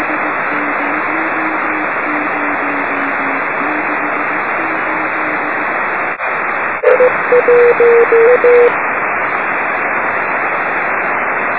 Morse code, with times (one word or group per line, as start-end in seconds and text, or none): none